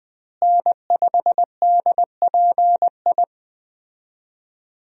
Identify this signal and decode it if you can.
Morse code, heard as a keyed tone, about 21 wpm, 700 Hz. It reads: N5DPI
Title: USA Call Signs - 5 Characters - Repeated 3x - 20wpm